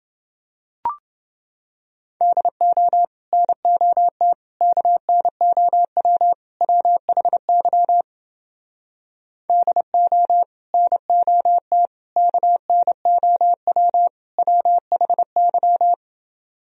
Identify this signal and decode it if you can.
Morse code, heard as a keyed tone, about 31 wpm, 700 Hz. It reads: DO NOT KNOW WHY DO NOT KNOW WHY